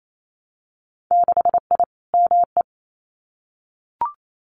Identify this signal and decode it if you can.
Morse code, heard as a keyed tone, about 28 wpm, 700 Hz. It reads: 6I ME
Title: Winter Field Day Exchanges Encore 28wpm